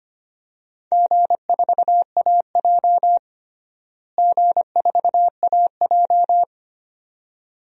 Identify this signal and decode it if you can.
Morse code, heard as a keyed tone, about 25 wpm, 700 Hz. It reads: G4AJ G4AJ
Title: DXCC-100 Call Signs - 4 Characters - Repeated 2x - 25wpm